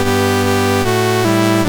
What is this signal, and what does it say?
There is no code.